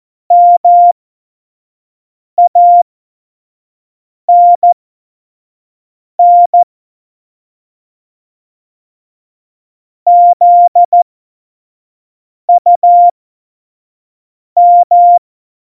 Morse code, heard as a keyed tone, about 14 wpm, spaced as for 5 wpm, 700 Hz. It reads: MANN ZUM
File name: Morsekurs / Witze / Witz 12. Dezember 2017, 25 F70